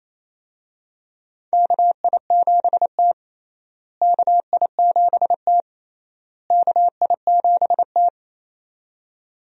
Morse code, heard as a keyed tone, about 28 wpm, 700 Hz. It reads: KI7T KI7T KI7T
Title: USA Call Signs - 4 Characters - Repeated 3x - 28wpm